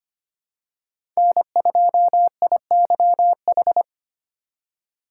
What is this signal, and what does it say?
Morse code, heard as a keyed tone, about 25 wpm, 700 Hz. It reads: N2IYH
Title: USA Call Signs - 5 Characters - 4x Character Spacing - 25wpm